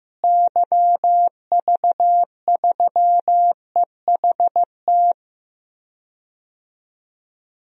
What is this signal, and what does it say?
Morse code, heard as a keyed tone, about 16 wpm, 700 Hz. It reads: YV3EHT
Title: DXCC-100 Call Signs - 6 Characters - Repeated 3x - 15wpm